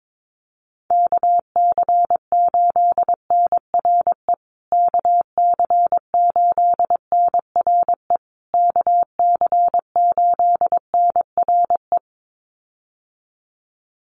Morse code, heard as a keyed tone, about 22 wpm, 700 Hz. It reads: KC8NRE KC8NRE KC8NRE
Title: USA Call Signs Repeated 3x 22wpm